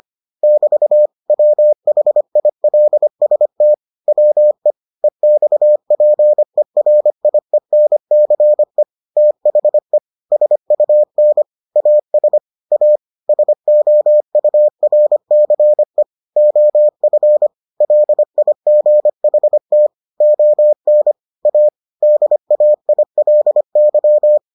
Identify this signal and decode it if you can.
Morse code, heard as a keyed tone, about 25 wpm, 600 Hz. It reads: <BT> WHILST WE EXPERIENCE THE SUN AS A SOURCE OF LIGHT ON A DAILY